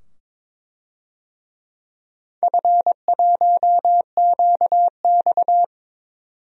Morse code, heard as a keyed tone, about 22 wpm, 700 Hz. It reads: F1QX
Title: DXCC-100 Call Signs - 4 Characters - 22wpm